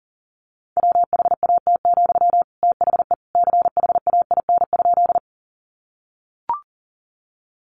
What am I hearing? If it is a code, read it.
Morse code, heard as a keyed tone, about 40 wpm, 700 Hz. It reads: WHAT, THE CHAIN?
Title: Sentences from Top 1500 Words 40wpm